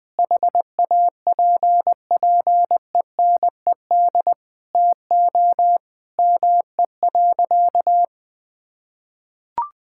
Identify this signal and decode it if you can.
Morse code, heard as a keyed tone, about 21 wpm, 700 Hz. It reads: HAPPENED TO ME.
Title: Sentences with 8-Word Phrases - 20wpm